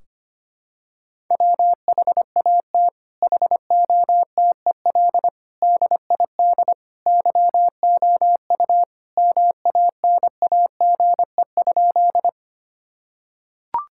Morse code, heard as a keyed tone, about 25 wpm, 700 Hz. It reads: WHAT HOTEL DID YOU MANAGE?
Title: Sentences from Top 1600 Words 25wpm